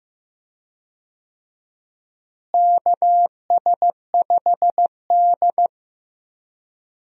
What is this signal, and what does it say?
Morse code, heard as a keyed tone, about 16 wpm, 700 Hz. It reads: KS5D
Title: USA Call Signs - 4 Characters - Repeated 3x - 15wpm